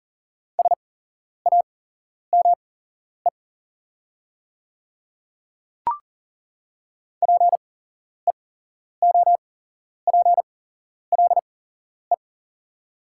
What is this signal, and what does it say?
Morse code, heard as a keyed tone, about 39 wpm, spaced as for 11 wpm, 700 Hz. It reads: SAME PEOPLE
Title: 8x Character Spacing - Top 200 Words - 40wpm